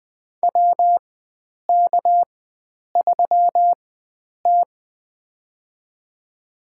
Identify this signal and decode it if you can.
Morse code, heard as a keyed tone, about 20 wpm, spaced as for 9 wpm, 700 Hz. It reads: WK3T
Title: USA Call Signs - 4 Characters - 4x Character Spacing - 20wpm